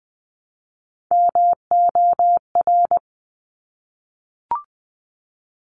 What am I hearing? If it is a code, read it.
Morse code, heard as a keyed tone, about 20 wpm, 700 Hz. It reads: MOR
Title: Top Three Letter Combinations in Words 20wpm